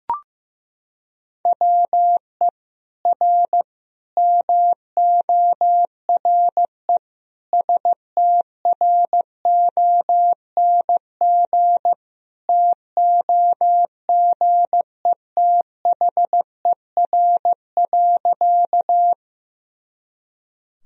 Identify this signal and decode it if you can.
Morse code, heard as a keyed tone, about 15 wpm, 700 Hz. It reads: WE R MORE STRONG TOGETHER.
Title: Sentences from Top 500 Words and Abbreviations 15wpm